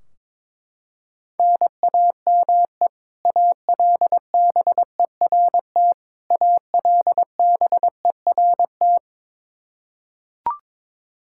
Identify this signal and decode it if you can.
Morse code, heard as a keyed tone, about 21 wpm, 700 Hz. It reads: NAME ALBERT ALBERT
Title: Practice QSO - Exchange 1 - Beginning - 22wpm